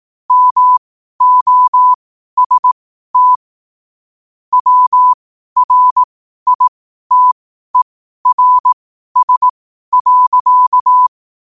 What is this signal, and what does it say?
Morse code, heard as a keyed tone, about 18 wpm, spaced as for 12 wpm, 1000 Hz. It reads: MOST WRITERS.
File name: qotdfiles / 12wpm